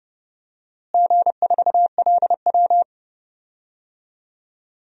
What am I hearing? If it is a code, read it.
Morse code, heard as a keyed tone, about 31 wpm, 700 Hz. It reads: G4LW
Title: DXCC-100 Call Signs - 4 Characters - Repeated 2x - 30wpm